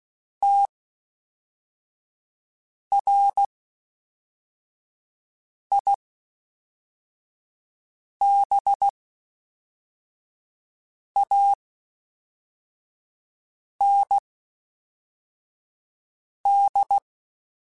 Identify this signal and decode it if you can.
Morse code, heard as a keyed tone, about 16 wpm, spaced as for 4 wpm, 800 Hz. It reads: TRIBAND